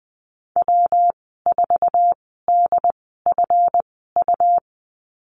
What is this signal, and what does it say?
Morse code, heard as a keyed tone, about 19 wpm, spaced as for 14 wpm, 700 Hz. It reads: W4DFU
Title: 167 Lesson 27g - Number 7 - Call Signs 20wpm